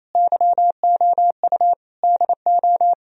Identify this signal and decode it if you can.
Morse code, heard as a keyed tone, about 35 wpm, 700 Hz. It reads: YOU DO